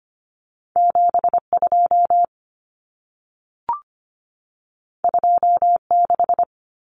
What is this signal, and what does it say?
Morse code, heard as a keyed tone, about 25 wpm, 700 Hz. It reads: 72 26
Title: Numbers - 2-Digit 25wpm